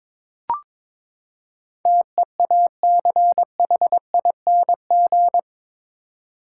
Morse code, heard as a keyed tone, about 22 wpm, 700 Hz. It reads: TEACHING
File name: practice / 22wpm / Eight Letter Words - 22wpm